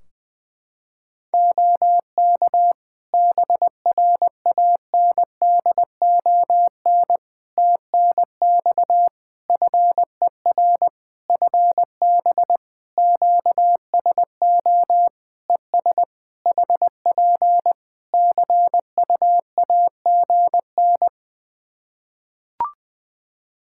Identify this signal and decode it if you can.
Morse code, heard as a keyed tone, about 20 wpm, 700 Hz. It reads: OK BRANDON TNX FER FB QSO ES HP CUAGN
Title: K1USN SST QSO Practice 20wpm